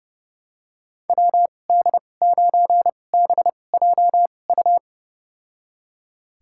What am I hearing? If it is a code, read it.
Morse code, heard as a keyed tone, about 31 wpm, spaced as for 22 wpm, 700 Hz. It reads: WD9BJU
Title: USA Call Signs - 6 Characters - 2x Character Spacing - 30wpm